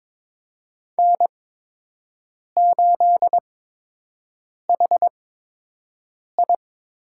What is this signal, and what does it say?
Morse code, heard as a keyed tone, about 21 wpm, spaced as for 6 wpm, 700 Hz. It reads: N8HI
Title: USA Call Signs - 8x Character Spacing - 22wpm